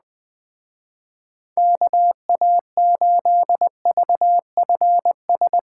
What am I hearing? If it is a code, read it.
Morse code, heard as a keyed tone, about 20 wpm, 700 Hz. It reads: KA8VFS
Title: Speed Racing - USA 2x3 Call Signs 17wpm